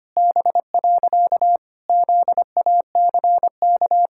Morse code, heard as a keyed tone, about 25 wpm, 700 Hz. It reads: B. ZACK
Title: Pangram Challenge v2 - 25wpm